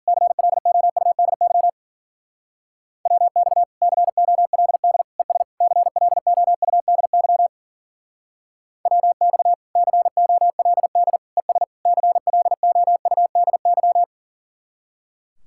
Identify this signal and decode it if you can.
Morse code, heard as a keyed tone, about 47 wpm, 700 Hz. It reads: CLOUDY WX COLD ES CLOUDY WX COLD ES CLOUDY
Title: Speed-Racing Common QSO Phrases - 40wpm